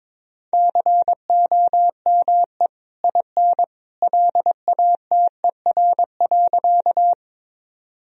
Morse code, heard as a keyed tone, about 22 wpm, 700 Hz. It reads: COME IN LATER.
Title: Sentences from Top 1000 Words - Review Part 4 - 22wpm